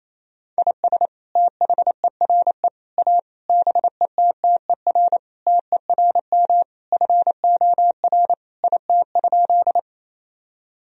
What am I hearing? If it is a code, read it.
Morse code, heard as a keyed tone, about 28 wpm, 700 Hz. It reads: IS THERE A BETTER TERM FOR IT?
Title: Sentences from Top 1000 Words - Review Part 4 - 28wpm